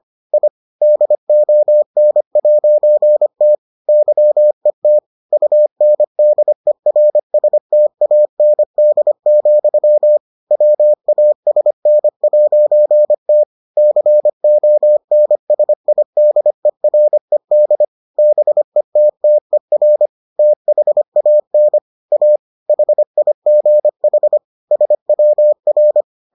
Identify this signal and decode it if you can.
Morse code, heard as a keyed tone, about 25 wpm, 600 Hz. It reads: I DON'T YET UNDERSTAND, WASN'T CONSIDERED BETTER THAN A HIGH SWR